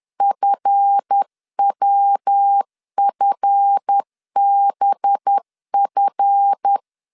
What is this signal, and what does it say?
Morse code, heard as a keyed tone, about 10 wpm, 800 Hz. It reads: FWFBF
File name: LEZIONI / 7 GF9Z / 7 GF 10wpm 2514